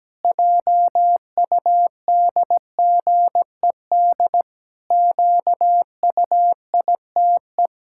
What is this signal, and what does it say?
Morse code, heard as a keyed tone, about 17 wpm, 700 Hz. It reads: JUDGED QUITE
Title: Pangram Challenge v2 - 17wpm